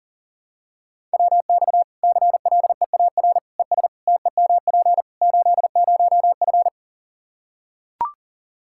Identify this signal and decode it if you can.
Morse code, heard as a keyed tone, about 40 wpm, 700 Hz. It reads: WX CLEAR ES TEMP 80F
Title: K1USN SST QSO Practice 40wpm